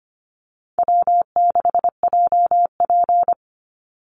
Morse code, heard as a keyed tone, about 25 wpm, 700 Hz. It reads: W6JP
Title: NA SOTA Chaser Callsigns 25wpm